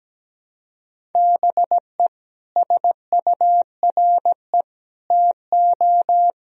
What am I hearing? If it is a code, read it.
Morse code, heard as a keyed tone, about 17 wpm, 700 Hz. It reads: BE SURE TO
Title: Sentences from Top 1000 Words - Review Part 4 - 17wpm